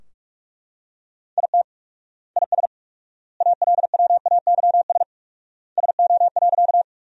Morse code, heard as a keyed tone, about 42 wpm, 700 Hz. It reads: IT IS ALWAYS SO.
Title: 4x Word-Spacing - Sentences from Top 300 Words - 45wpm